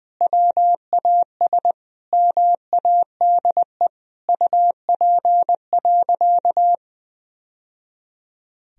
Morse code, heard as a keyed tone, about 20 wpm, 700 Hz. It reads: WAS MADE UP.
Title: Sentences from Top 1000 Words - Review Part 2 - 20wpm